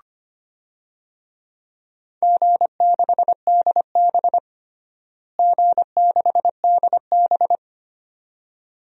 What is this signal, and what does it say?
Morse code, heard as a keyed tone, about 25 wpm, 700 Hz. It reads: G6DB G6DB
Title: DXCC-100 Call Signs - 4 Characters - Repeated 2x - 25wpm